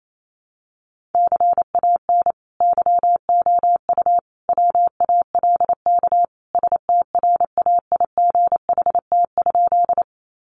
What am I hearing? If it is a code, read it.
Morse code, heard as a keyed tone, about 28 wpm, 700 Hz. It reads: CAN YOU WALK STRAIGHT?
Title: Sentences from Top 1000 Words 28wpm